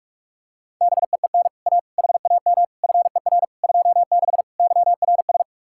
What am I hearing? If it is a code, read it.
Morse code, heard as a keyed tone, about 38 wpm, 700 Hz. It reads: BEEN A HAM FER 26 YRS